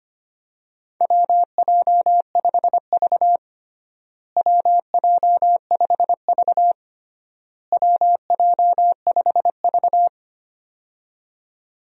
Morse code, heard as a keyed tone, about 25 wpm, 700 Hz. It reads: WJ5V WJ5V WJ5V
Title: USA Call Signs - 4 Characters - Repeated 3x - 25wpm